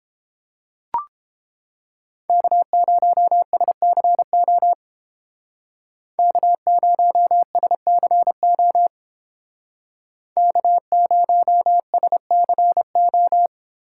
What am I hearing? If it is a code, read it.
Morse code, heard as a keyed tone, about 28 wpm, 700 Hz. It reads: K0SCO K0SCO K0SCO